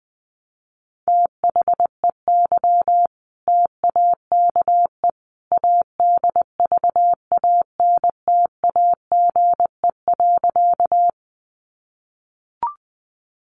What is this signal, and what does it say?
Morse code, heard as a keyed tone, about 20 wpm, 700 Hz. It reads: THEY TAKE ADVANTAGE.
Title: Sentences from Top 900 Words 20wpm